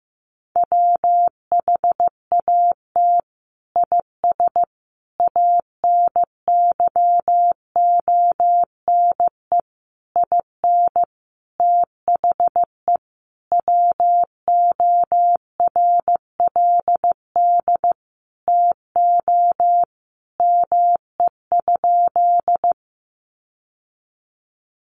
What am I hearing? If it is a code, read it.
Morse code, heard as a keyed tone, about 15 wpm, 700 Hz. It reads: WHAT IS ANYONE IN THE WORLD TO ME?
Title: Sentences from Top 900 Words 15wpm